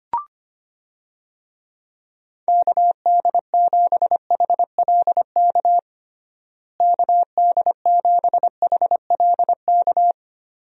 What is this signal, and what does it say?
Morse code, heard as a keyed tone, about 25 wpm, 700 Hz. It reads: KD7HLK KD7HLK
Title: USA Call Signs - 6 Characters - Repeated 2x - 25wpm